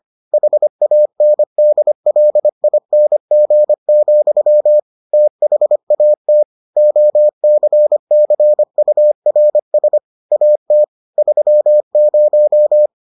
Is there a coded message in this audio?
Yes: Morse code from 0.3 to 12.9 s, heard as a keyed tone, about 25 wpm, 600 Hz. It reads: HANDLING, THAT OCCURS AT 30